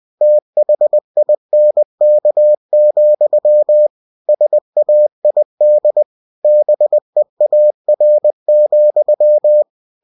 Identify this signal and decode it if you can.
Morse code, heard as a keyed tone, about 20 wpm, 600 Hz. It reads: THINK, SAID BEAR,
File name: cw-books / winnie-the-pooh / winnie-20wpm-page-0017